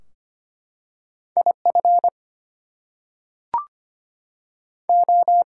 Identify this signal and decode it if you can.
Morse code, heard as a keyed tone, about 25 wpm, 700 Hz. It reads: IF O
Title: Two Letter Words - 25wpm